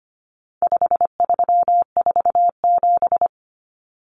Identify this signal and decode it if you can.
Morse code, heard as a keyed tone, about 25 wpm, 700 Hz. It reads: 5347
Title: Common Cut Numbers - 4-Digit 25wpm